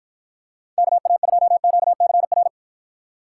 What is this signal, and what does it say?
Morse code, heard as a keyed tone, about 53 wpm, 700 Hz. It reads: KA2QCR